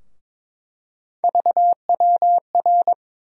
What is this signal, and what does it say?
Morse code, heard as a keyed tone, about 22 wpm, 700 Hz. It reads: VWR
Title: USA Call Sign Postfixes - 3 Letters 22wpm